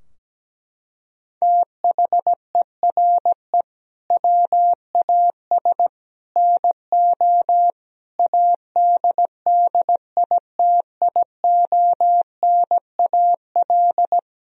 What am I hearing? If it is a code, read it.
Morse code, heard as a keyed tone, about 17 wpm, 700 Hz. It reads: THERE WAS NO ADDITIONAL
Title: Sentences from Top 1000 Words - Review Part 4 - 17wpm